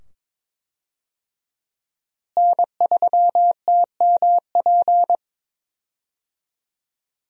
Morse code, heard as a keyed tone, about 22 wpm, 700 Hz. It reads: N3TMP
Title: USA Call Signs - 5 Characters - Repeated 2x - 22wpm